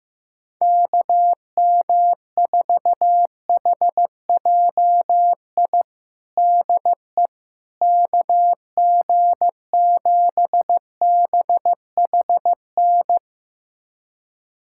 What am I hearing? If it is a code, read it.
Morse code, heard as a keyed tone, about 15 wpm, 700 Hz. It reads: KM4HJI DE KG7BHN
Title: USA Callsign DE Callsign - 6 Characters - 15wpm